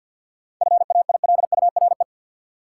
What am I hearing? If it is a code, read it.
Morse code, heard as a keyed tone, about 53 wpm, 700 Hz. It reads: FAILURE
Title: Sentences from Top 900 Words - New Word Drill - 50wpm